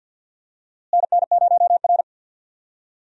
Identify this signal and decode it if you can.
Morse code, heard as a keyed tone, about 52 wpm, 700 Hz. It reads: NN0R
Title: USA Call Signs - 4 Characters - Repeated 2x - 50wpm